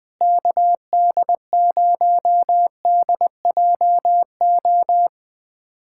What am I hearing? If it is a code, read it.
Morse code, heard as a keyed tone, about 21 wpm, 700 Hz. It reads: KD0DJO
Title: USA Callsign DE Callsign 20wpm